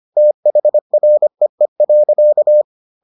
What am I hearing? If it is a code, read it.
Morse code, heard as a keyed tone, about 25 wpm, 600 Hz. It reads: THREE.